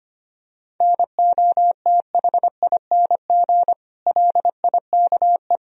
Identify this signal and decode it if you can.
Morse code, heard as a keyed tone, about 25 wpm, 700 Hz. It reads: NOTHING LIKE
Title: Sentences from Top 1000 Words - Review Part 4 - 25wpm